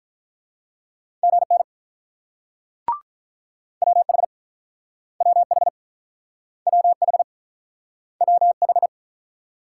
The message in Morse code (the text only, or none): GN WH WH WH WH